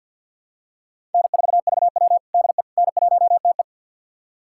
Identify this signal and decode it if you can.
Morse code, heard as a keyed tone, about 47 wpm, 700 Hz. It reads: N4VW DE N1TE
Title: USA Callsign DE Callsign - 4 Characters - 50wpm